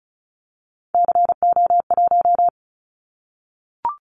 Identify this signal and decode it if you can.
Morse code, heard as a keyed tone, about 34 wpm, 700 Hz. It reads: CO1
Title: Callsign Prefixes for Top 100 DXCC - 35wpm